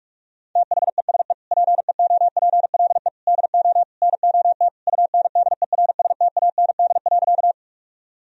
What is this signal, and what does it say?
Morse code, heard as a keyed tone, about 41 wpm, 700 Hz. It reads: THESE PEOPLE DO NOT UNDERSTAND.
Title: Sentences from Top 1000 Words - Review Part 1 - 45wpm